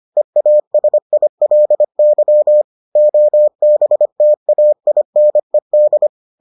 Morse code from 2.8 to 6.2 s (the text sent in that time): OBTAINED